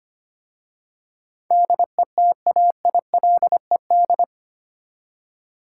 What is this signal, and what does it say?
Morse code, heard as a keyed tone, about 25 wpm, 700 Hz. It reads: DETAILED
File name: 8 Letter Words - First 3 Letters Given - 25wpm